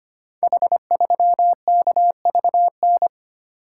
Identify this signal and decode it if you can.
Morse code, heard as a keyed tone, about 25 wpm, 700 Hz. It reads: H3KVN
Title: Code Groups - 5 Letter:Numbers - 25wpm